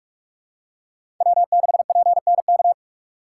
Morse code, heard as a keyed tone, about 46 wpm, 700 Hz. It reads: W6PNK
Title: USA Call Signs - 3x Character Spacing - 45wpm